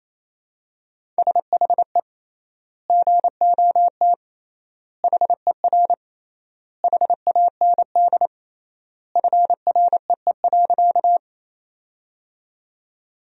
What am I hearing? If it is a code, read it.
Morse code, heard as a keyed tone, about 28 wpm, 700 Hz. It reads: SHE GOT HER HAND FREE.
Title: 3x Word-Spacing - Sentences from Top 500 Words - 28wpm